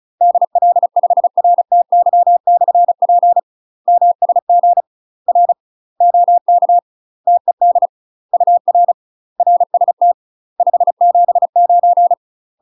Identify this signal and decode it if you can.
Morse code, heard as a keyed tone, about 35 wpm, 700 Hz. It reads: DL5RTY/P MSG R OK TED UR RST 579
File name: UK MASTER QSOs TRAINING / 35WPM / UK MASTER QSOs 0039